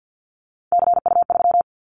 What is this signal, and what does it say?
Morse code, heard as a keyed tone, about 50 wpm, 700 Hz. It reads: CU3
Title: Callsign Prefixes for Top 100 DXCC - 50wpm